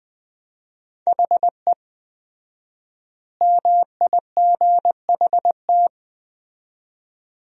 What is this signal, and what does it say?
Morse code, heard as a keyed tone, about 20 wpm, 700 Hz. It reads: HE MIGHT
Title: 4x Word-Spacing - Sentences from Top 300 Words - 20wpm